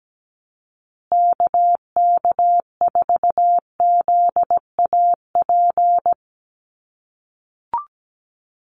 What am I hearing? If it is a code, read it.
Morse code, heard as a keyed tone, about 17 wpm, 700 Hz. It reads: KK4ZAP
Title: USA Call Signs Repeated 2x 17wpm